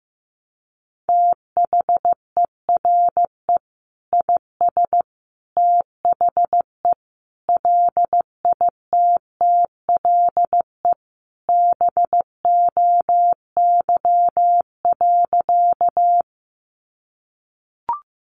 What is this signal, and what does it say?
Morse code, heard as a keyed tone, about 15 wpm, 700 Hz. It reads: THERE IS THE LITTLE BOY.
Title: Sentences from Top 200 Words 15wpm